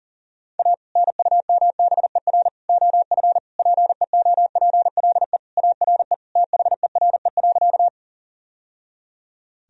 Morse code, heard as a keyed tone, about 40 wpm, 700 Hz. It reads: A NUMBER OF PEOPLE ARE THERE.
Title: Sentences from Top 1000 Words - Review Part 1 - 40wpm